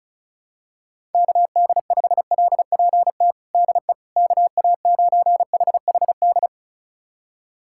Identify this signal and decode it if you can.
Morse code, heard as a keyed tone, about 34 wpm, 700 Hz. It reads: KD5LPT DE KA9HHD
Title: USA Callsign DE Callsign 35wpm